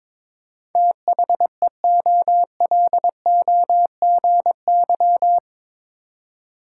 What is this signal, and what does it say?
Morse code, heard as a keyed tone, about 22 wpm, 700 Hz. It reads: THEOLOGY
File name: practice / 22wpm / Eight Letter Words - 22wpm